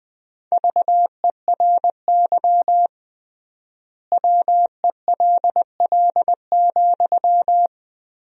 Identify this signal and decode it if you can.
Morse code, heard as a keyed tone, about 20 wpm, 700 Hz. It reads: VERY WELL,
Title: 3x Word-Spacing - Sentences from Top 500 Words - 20wpm